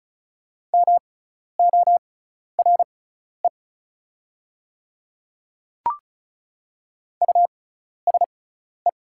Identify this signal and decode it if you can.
Morse code, heard as a keyed tone, about 34 wpm, spaced as for 12 wpm, 700 Hz. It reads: MORE USE
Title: 6x Character Spacing - Top 100 Words - 35wpm